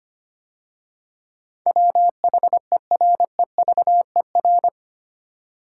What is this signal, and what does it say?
Morse code, heard as a keyed tone, about 25 wpm, 690 Hz. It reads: WHEREVER
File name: 8 Letter Words - First 3 Letters Given - 25wpm